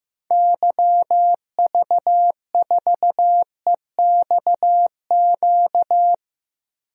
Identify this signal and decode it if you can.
Morse code, heard as a keyed tone, about 15 wpm, 700 Hz. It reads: YV4EXQ
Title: DXCC-100 Call Signs - 6 Characters - Repeated 2x - 15wpm